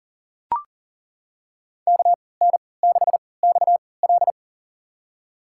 Morse code, heard as a keyed tone, about 40 wpm, spaced as for 23 wpm, 700 Hz. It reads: KN6XL
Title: USA Call Signs - 5 Characters - 3x Character Spacing - 40wpm